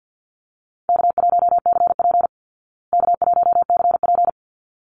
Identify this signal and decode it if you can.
Morse code, heard as a keyed tone, about 48 wpm, 700 Hz. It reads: K1CP K1CP
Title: Easy Call Signs Repeated 2x - 50wpm